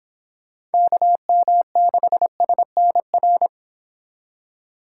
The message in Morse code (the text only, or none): KM6SNR